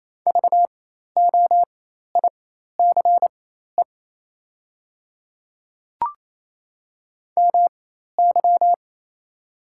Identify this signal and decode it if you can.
Morse code, heard as a keyed tone, about 28 wpm, spaced as for 13 wpm, 700 Hz. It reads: VOICE MY
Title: 4x Character Spacing - Top 500 Words - 28wpm